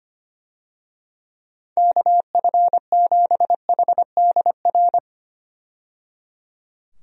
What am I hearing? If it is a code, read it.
Morse code, heard as a keyed tone, about 25 wpm, 700 Hz. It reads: KF7HDR
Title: USA Call Signs - 6 Characters - 25wpm